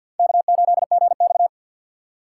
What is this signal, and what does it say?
Morse code, heard as a keyed tone, about 47 wpm, 700 Hz. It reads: K8GX
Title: USA Call Signs - 4 Characters - 6x Character Spacing - 50wpm